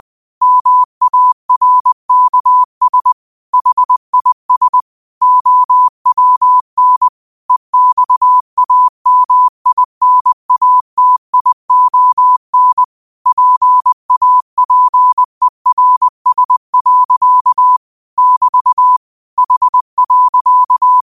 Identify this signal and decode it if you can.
Morse code, heard as a keyed tone, about 20 wpm, 1000 Hz. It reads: MARKS HIS OWN EXAMINATION PAPERS. <BT> H.
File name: qotdfiles / 20wpm